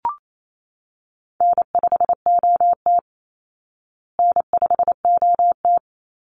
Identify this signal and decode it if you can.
Morse code, heard as a keyed tone, about 28 wpm, 700 Hz. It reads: N5OT N5OT
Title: Easy Call Signs Repeated 3x - 28wpm